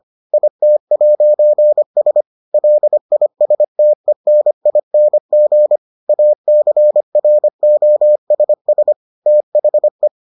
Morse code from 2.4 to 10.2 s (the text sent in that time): LISTENING ACROSS THE